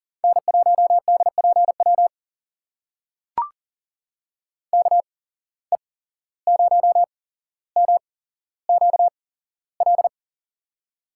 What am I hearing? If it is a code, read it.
Morse code, heard as a keyed tone, about 40 wpm, spaced as for 16 wpm, 700 Hz. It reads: N1DPW KE0MQL
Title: USA Call Signs - 8x Character Spacing - 40wpm